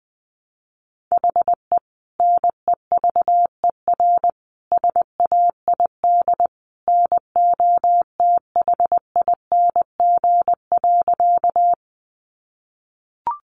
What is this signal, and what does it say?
Morse code, heard as a keyed tone, about 20 wpm, 700 Hz. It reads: HE NEVER SAID NOTHING.
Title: Sentences from Top 500 Words 20wpm